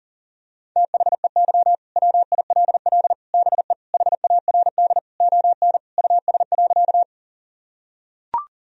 Morse code, heard as a keyed tone, about 37 wpm, 700 Hz. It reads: THEY WILL BE HARD ON US.
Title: Sentences from Top 1000 Words - Review Part 1 - 40wpm